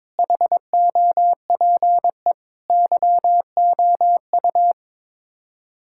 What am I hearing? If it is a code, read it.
Morse code, heard as a keyed tone, about 22 wpm, 700 Hz. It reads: HOPE YOU
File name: practice / 22wpm / Sets of 8 Words - 22wpm